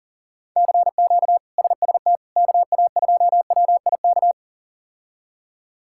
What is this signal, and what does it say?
Morse code, heard as a keyed tone, about 40 wpm, 700 Hz. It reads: CQ SST KA2WIK
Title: K1USN SST QSO Practice - 40wpm